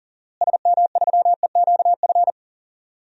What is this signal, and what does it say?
Morse code, heard as a keyed tone, about 40 wpm, 700 Hz. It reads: SM3EQF
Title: DXCC-100 Call Signs - 6 Characters - Repeated 3x - 40wpm